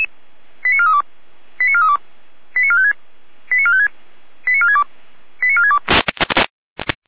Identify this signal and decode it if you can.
Five-tone selective call (ZVEI), heard as quick runs of five tones, tones of about 70 ms. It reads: 89432 89432 89456 89456 89462 89462
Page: /455/SDR/ZVEI/89417/